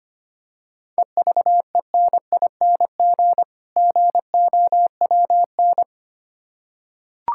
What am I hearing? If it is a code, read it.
Morse code, heard as a keyed tone, about 25 wpm, 700 Hz. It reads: EVENING GOWN
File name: practice / 25wpm / Difficult Words with Hints - 25wpm